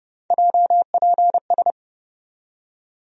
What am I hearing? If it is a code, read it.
Morse code, heard as a keyed tone, about 30 wpm, 700 Hz. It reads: JPS
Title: USA Call Sign Postfixes - 3 Letters 30wpm